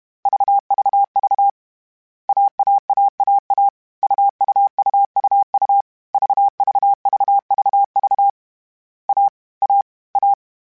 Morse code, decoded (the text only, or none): VVV AAAAA UUUUU VVVVV A A A